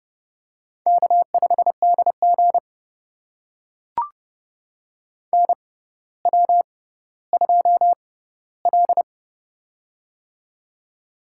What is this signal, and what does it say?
Morse code, heard as a keyed tone, about 29 wpm, spaced as for 15 wpm, 700 Hz. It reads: K5DG NW2L
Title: USA Call Signs - 6x Character Spacing - 30wpm